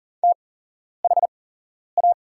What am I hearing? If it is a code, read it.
Morse code, heard as a keyed tone, about 40 wpm, spaced as for 11 wpm, 700 Hz. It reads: THA
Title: 8x Character Spacing - Top 100 Words - 40wpm